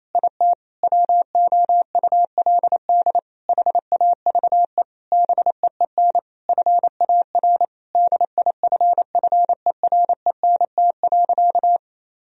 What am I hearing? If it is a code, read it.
Morse code, heard as a keyed tone, about 28 wpm, 700 Hz. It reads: IT WOULD HAVE BEEN FAR DIFFERENT.
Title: Sentences from Top 1000 Words - Review Part 3 - 28wpm